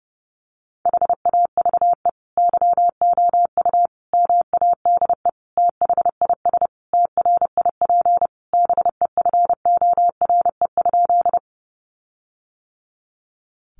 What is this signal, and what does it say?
Morse code, heard as a keyed tone, about 30 wpm, 700 Hz. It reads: HAVE YOU MADE THIS TRIP BEFORE?
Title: Sentences from Top 1300 Words 30wpm